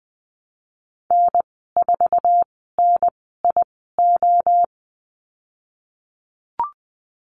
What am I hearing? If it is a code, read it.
Morse code, heard as a keyed tone, about 20 wpm, spaced as for 15 wpm, 700 Hz. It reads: N4NIO
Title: 048 Lesson 9g - Number 4 - Call Signs 20wpm